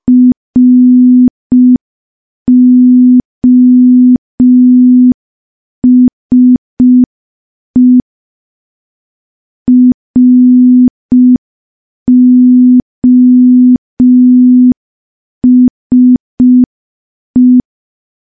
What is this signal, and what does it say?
Morse code, heard as a keyed tone, about 5 wpm, 250 Hz. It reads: ROSE ROSE
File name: morse / morse-training-05WPM